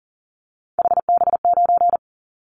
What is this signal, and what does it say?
Morse code, heard as a keyed tone, about 37 wpm, 700 Hz. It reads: HB9
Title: Callsign Prefixes for Top 25 DXCC - Repeated 3x - 40wpm